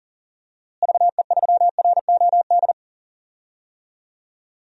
Morse code, heard as a keyed tone, about 40 wpm, 700 Hz. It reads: VE3ROD
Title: DXCC-100 Call Signs - 6 Characters - Repeated 3x - 40wpm